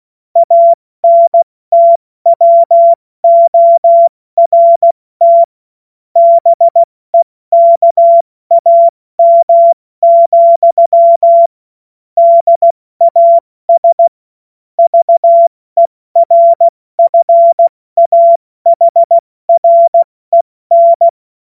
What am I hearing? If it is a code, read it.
Morse code, heard as a keyed tone, about 16 wpm, spaced as for 14 wpm, 680 Hz. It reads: ANTWORT BEKAM, DAS VERFAHREN
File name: Morsekurs / Witze / Witz 30. August 2022 70 F80